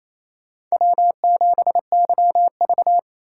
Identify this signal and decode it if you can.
Morse code, heard as a keyed tone, about 28 wpm, 700 Hz. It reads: W7YV
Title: USA Call Signs - 4 Characters - 4x Character Spacing - 28wpm